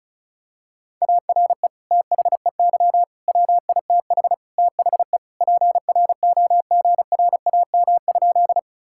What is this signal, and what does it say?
Morse code, heard as a keyed tone, about 36 wpm, 700 Hz. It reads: ARE THEY WITH THE PROGRAM?
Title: Sentences from Top 1000 Words - Review Part 5 - 35wpm